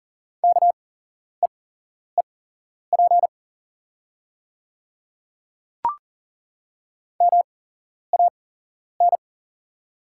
Morse code, heard as a keyed tone, about 37 wpm, spaced as for 11 wpm, 690 Hz. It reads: KEEP MAN
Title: 8x Character Spacing - Top 500 Words - 40wpm